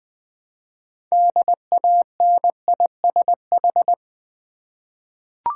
Morse code, heard as a keyed tone, about 20 wpm, 700 Hz. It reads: DANISH
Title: Difficult Words with Hints - 20wpm